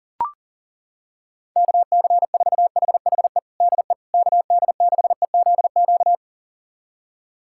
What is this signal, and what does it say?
Morse code, heard as a keyed tone, about 37 wpm, 700 Hz. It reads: KC4HHE DE KD6EZQ
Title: USA Callsign DE Callsign - 6 Characters - 40wpm